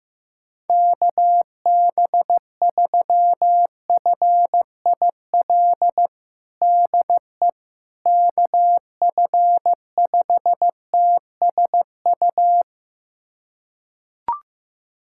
Morse code, heard as a keyed tone, about 15 wpm, 700 Hz. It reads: KB3FIL DE KF5TSU
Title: USA Callsign DE Callsign - 6 Characters - 15wpm